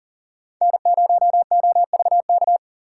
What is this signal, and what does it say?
Morse code, heard as a keyed tone, about 40 wpm, 700 Hz. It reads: N0OVK
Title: USA Call Signs - 4x Character Spacing - 40wpm